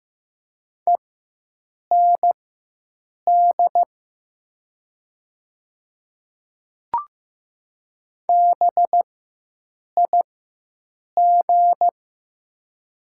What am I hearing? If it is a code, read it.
Morse code, heard as a keyed tone, about 15 wpm, spaced as for 7 wpm, 700 Hz. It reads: END BIG
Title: 4x Character Spacing - Top 300 Words - 15wpm